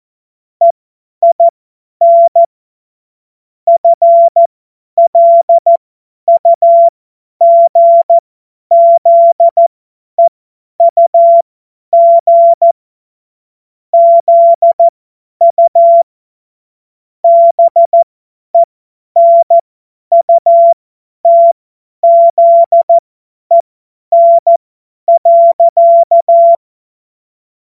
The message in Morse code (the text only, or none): EIN FLUGZEUG ZU BENUTZEN.